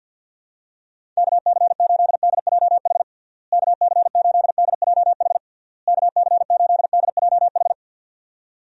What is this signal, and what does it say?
Morse code, heard as a keyed tone, about 52 wpm, 700 Hz. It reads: KC8DJH KC8DJH KC8DJH
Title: USA Call Signs - 6 Characters - Repeated 3x - 50wpm